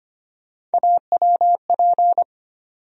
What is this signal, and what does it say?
Morse code, heard as a keyed tone, about 25 wpm, 700 Hz. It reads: AWP
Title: USA Call Sign Postfixes - 3 Letters 25wpm